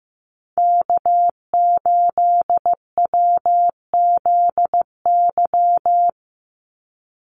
Morse code, heard as a keyed tone, about 15 wpm, 700 Hz. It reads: K8WZY
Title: USA Call Signs 15wpm